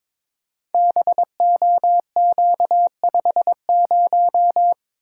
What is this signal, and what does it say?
Morse code, heard as a keyed tone, about 22 wpm, 700 Hz. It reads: BOQ50
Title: Code Groups - 5 Letter:Numbers - 22wpm